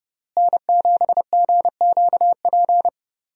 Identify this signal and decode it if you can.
Morse code, heard as a keyed tone, about 30 wpm, 700 Hz. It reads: N7GQP
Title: USA Call Signs - 5 Characters - Repeated 3x - 30wpm